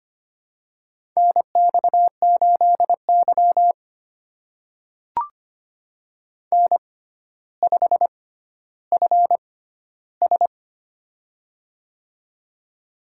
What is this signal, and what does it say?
Morse code, heard as a keyed tone, about 25 wpm, spaced as for 13 wpm, 700 Hz. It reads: NX8Y N5FS
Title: USA Call Signs - 4 Characters - 6x Character Spacing - 25wpm